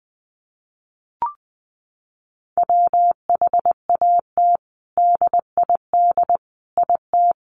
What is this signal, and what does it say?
Morse code, heard as a keyed tone, about 20 wpm, 700 Hz. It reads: WHAT DID IT
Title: Sentences from Top 1400 Words 20wpm